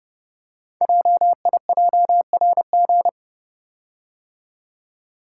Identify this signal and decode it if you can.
Morse code, heard as a keyed tone, about 30 wpm, 700 Hz. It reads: JIJRG
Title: Code Groups - 5 Letters - 30wpm